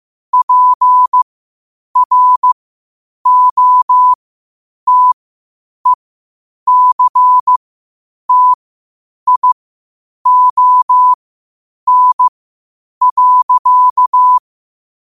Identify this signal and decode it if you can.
Morse code, heard as a keyed tone, about 15 wpm, spaced as for 8 wpm, 1000 Hz. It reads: PROTECTION.
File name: qotdfiles / 8wpm